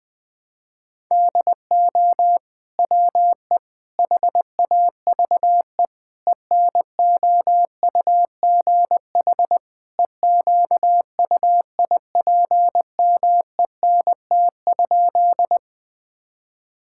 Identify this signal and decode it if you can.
Morse code, heard as a keyed tone, about 20 wpm, 700 Hz. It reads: DO WE HAVE ENOUGH EQUIPMENT?
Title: Sentences from Top 1000 Words - Review Part 5 - 20wpm